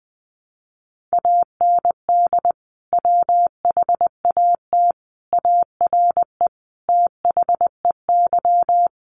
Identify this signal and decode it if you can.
Morse code, heard as a keyed tone, about 20 wpm, 700 Hz. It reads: AND WHAT ARE THEY
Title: Sentences from Top 1100 Words 20wpm